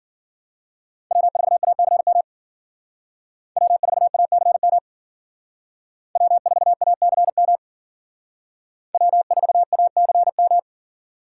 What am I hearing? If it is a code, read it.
Morse code, heard as a keyed tone, about 52 wpm, 700 Hz. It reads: W4ACM W4ACM W4ACM W4ACM